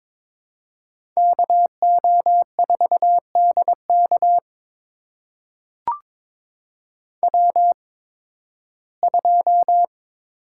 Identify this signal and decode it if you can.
Morse code, heard as a keyed tone, about 22 wpm, spaced as for 14 wpm, 700 Hz. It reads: KO4DK W2
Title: USA Call Signs - 5 Characters - 8x Character Spacing - 22wpm